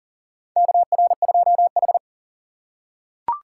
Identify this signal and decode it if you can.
Morse code, heard as a keyed tone, about 40 wpm, 700 Hz. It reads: KR2H
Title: USA Call Signs - 4 Characters - 4x Character Spacing - 40wpm